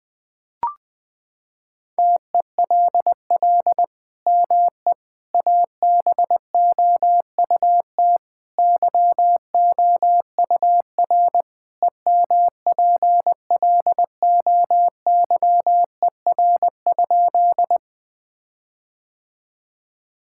Morse code, heard as a keyed tone, about 20 wpm, 700 Hz. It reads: TELL ME ABOUT YOUR EMPLOYER?
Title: Sentences from Top 1600 Words 20wpm